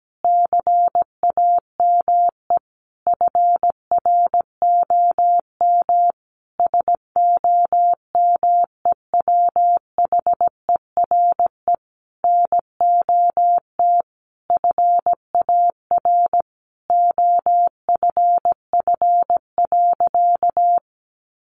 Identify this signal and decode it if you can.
Morse code, heard as a keyed tone, about 17 wpm, 700 Hz. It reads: CAME FROM SOMEWHERE NOT FAR OFF.
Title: Sentences from Top 1400 Words 17wpm